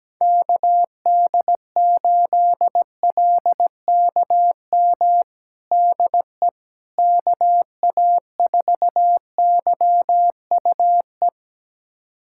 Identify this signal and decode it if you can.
Morse code, heard as a keyed tone, about 17 wpm, 700 Hz. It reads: KD8LKM DE KA4YUE
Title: USA Callsign DE Callsign - 6 Characters - 17wpm